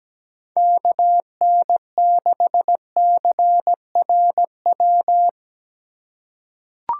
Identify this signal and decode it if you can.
Morse code, heard as a keyed tone, about 17 wpm, 700 Hz. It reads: KN6CRW
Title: USA Call Signs - 4x Character Spacing - 17wpm